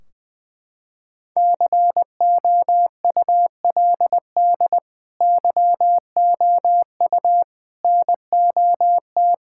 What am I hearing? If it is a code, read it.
Morse code, heard as a keyed tone, about 21 wpm, 700 Hz. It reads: COULD YOU NOT
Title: Sentences from Top 1600 Words 20wpm